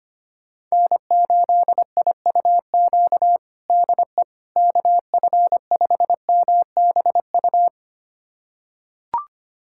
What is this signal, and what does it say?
Morse code, heard as a keyed tone, about 25 wpm, 700 Hz. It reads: N8IUQ DE KF5MBU
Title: USA Callsign DE Callsign 25wpm